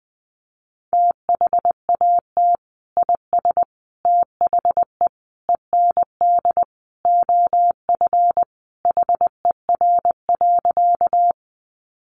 Morse code, heard as a keyed tone, about 20 wpm, 700 Hz. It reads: THAT IS THE END OF HER.
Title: Sentences from Top 200 Words 20wpm